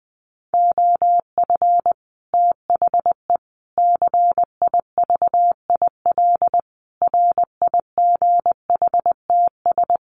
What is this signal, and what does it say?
Morse code, heard as a keyed tone, about 19 wpm, 700 Hz. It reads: OF THE CIVIL RIGHTS
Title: Sets of 5 Words Encore2 20wpm